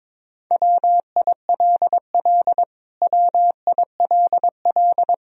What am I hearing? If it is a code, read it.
Morse code, heard as a keyed tone, about 22 wpm, 700 Hz. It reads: WILL WILL
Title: Top 100 Words Repeated 22wpm